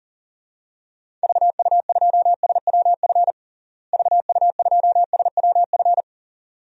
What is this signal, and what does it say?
Morse code, heard as a keyed tone, about 40 wpm, 700 Hz. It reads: VU2SWF VU2SWF
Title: DXCC-100 Call Signs - 6 Characters - Repeated 2x - 40wpm